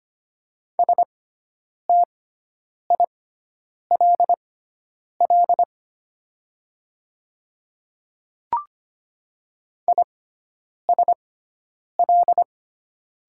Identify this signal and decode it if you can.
Morse code, heard as a keyed tone, about 25 wpm, spaced as for 9 wpm, 700 Hz. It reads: STILL ISL
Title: 6x Character Spacing - Top 500 Words - 25wpm